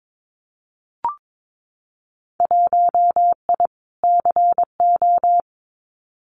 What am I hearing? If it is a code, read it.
Morse code, heard as a keyed tone, about 22 wpm, 700 Hz. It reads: 1I CO
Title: Winter Field Day Exchanges Encore 22wpm